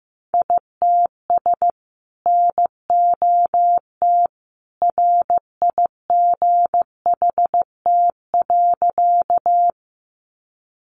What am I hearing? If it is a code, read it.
Morse code, heard as a keyed tone, about 15 wpm, 700 Hz. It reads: ITS NOT RIGHT.
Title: Sentences from Top 600 Words 15wpm